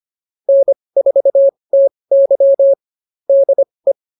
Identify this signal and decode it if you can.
Morse code, heard as a keyed tone, about 24 wpm, spaced as for 20 wpm, 550 Hz. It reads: N4TY DE
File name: QSO109a 20